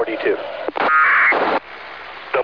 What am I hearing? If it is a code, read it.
Packet radio (AX.25, as in APRS), with speech.
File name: ho-68-12-24-2009